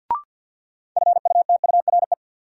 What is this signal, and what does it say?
Morse code, heard as a keyed tone, about 53 wpm, 700 Hz. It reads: FUTURE
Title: Sets of 3 Words - New Word Drill - 50wpm